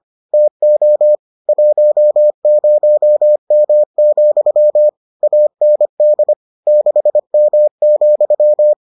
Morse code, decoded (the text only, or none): TO 10M, AND 6M,